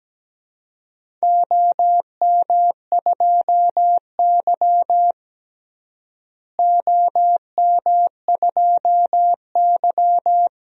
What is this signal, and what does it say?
Morse code, heard as a keyed tone, about 17 wpm, 700 Hz. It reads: OM2Y OM2Y
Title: DXCC-100 Call Signs - 4 Characters - Repeated 2x - 17wpm